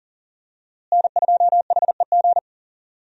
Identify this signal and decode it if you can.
Morse code, heard as a keyed tone, about 40 wpm, 700 Hz. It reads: N2HEG
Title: USA Call Signs - 5 Characters - Repeated 3x - 40wpm